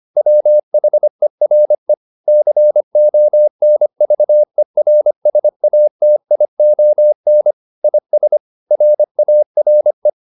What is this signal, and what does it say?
Morse code, heard as a keyed tone, about 25 wpm, 600 Hz. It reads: WHERE CONVERSATION IS RARE